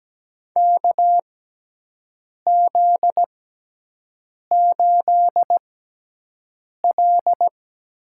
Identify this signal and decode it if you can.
Morse code, heard as a keyed tone, about 17 wpm, spaced as for 6 wpm, 700 Hz. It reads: KZ8L
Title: USA Call Signs - 4 Characters - 6x Character Spacing - 17wpm